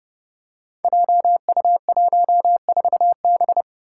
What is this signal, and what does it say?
Morse code, heard as a keyed tone, about 30 wpm, 700 Hz. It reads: JU14B
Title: Five Challenging Letter-Number Combinations 30wpm